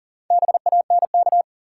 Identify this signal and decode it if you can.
Morse code, heard as a keyed tone, about 39 wpm, 700 Hz. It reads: BANK
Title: Sentences from Top 900 Words - New Word Drill - 40wpm